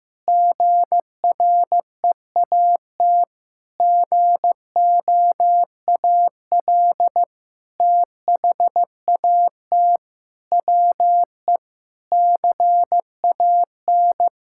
Morse code, heard as a keyed tone, about 16 wpm, 700 Hz. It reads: GREAT GOAL THAT WE CAN
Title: Sentences from Top 1000 Words - Review Part 6 - 15wpm